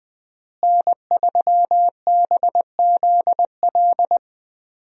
Morse code, heard as a keyed tone, about 20 wpm, 700 Hz. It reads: N3BZL
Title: USA Call Signs - 5 Characters - Repeated 3x - 20wpm